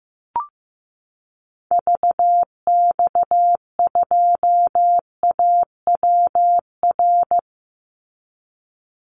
Morse code, heard as a keyed tone, about 15 wpm, 700 Hz. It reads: VX2AWR
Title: Call Signs Repeated 2x - 15wpm